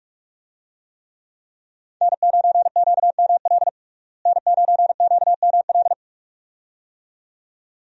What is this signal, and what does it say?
Morse code, heard as a keyed tone, about 43 wpm, 700 Hz. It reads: N9QML N9QML
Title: USA Call Signs - 5 Characters - Repeated 2x - 45wpm